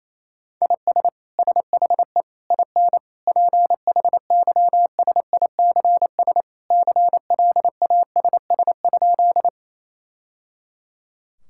Morse code, heard as a keyed tone, about 28 wpm, 700 Hz. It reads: IS SHE IN PHYSICS CLASS?
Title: Sentences from Top 1000 Words - Review Part 5 - 28wpm